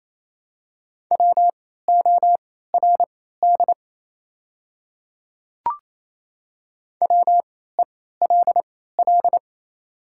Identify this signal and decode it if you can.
Morse code, heard as a keyed tone, about 28 wpm, spaced as for 16 wpm, 700 Hz. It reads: WORD WELL
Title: 3x Character Spacing - Top 500 Words - 28wpm